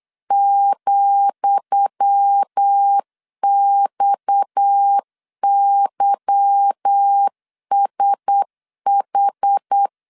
Morse code, heard as a keyed tone, about 8 wpm, 800 Hz. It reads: ,XYSH
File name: LEZIONI / 10 SIGNS / 10D COMA 8wpm 3749